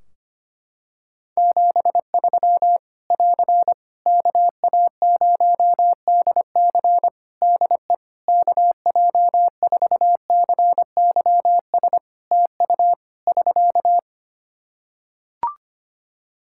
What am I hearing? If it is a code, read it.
Morse code, heard as a keyed tone, about 25 wpm, 700 Hz. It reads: 73 <AR> KA0DC DE KJ4CYS TU <SK>
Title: K1USN SST QSO Practice 25wpm